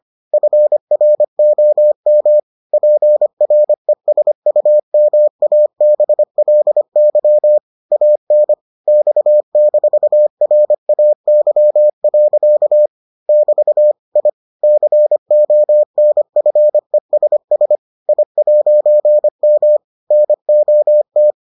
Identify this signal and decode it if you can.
Morse code, heard as a keyed tone, about 25 wpm, 600 Hz. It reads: FROM PRESUMABLY AN X-RAY. <BT> I CONFESS I'M NOT